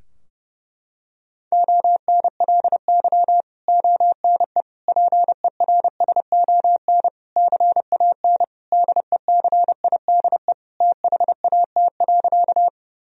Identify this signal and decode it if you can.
Morse code, heard as a keyed tone, about 32 wpm, 700 Hz. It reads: ONLY ONE PERSON CAN DECIDE THAT.
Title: Sentences from Top 1000 Words - Review Part 3 - 30wpm